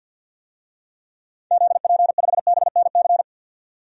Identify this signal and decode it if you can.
Morse code, heard as a keyed tone, about 50 wpm, 700 Hz. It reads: ZP5BNC
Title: DXCC-100 Call Signs - 6 Characters - 50wpm